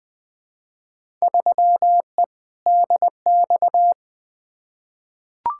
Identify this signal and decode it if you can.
Morse code, heard as a keyed tone, about 20 wpm, 700 Hz. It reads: 3E DX
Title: Field Day Exchanges Encore - 20wpm